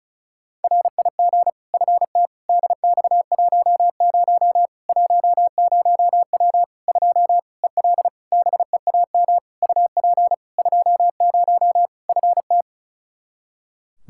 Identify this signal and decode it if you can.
Morse code, heard as a keyed tone, about 36 wpm, 700 Hz. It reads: RIG FT DX10 10W 2 EL BEAM UP 20 FT
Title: Practice QSO - Exchange 2 - Info - 35wpm